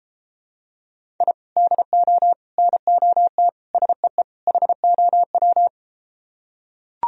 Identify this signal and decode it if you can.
Morse code, heard as a keyed tone, about 33 wpm, 700 Hz. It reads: I DO NOT SEE HOW